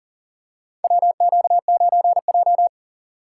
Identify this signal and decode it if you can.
Morse code, heard as a keyed tone, about 37 wpm, 700 Hz. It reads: WQ9J
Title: USA Call Signs - 4 Characters - 3x Character Spacing - 40wpm